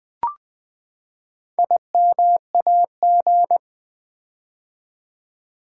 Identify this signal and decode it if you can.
Morse code, heard as a keyed tone, about 20 wpm, 700 Hz. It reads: IMAG
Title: Build A Word - 4 Letters At A Time - 11-Letter Words - 20wpm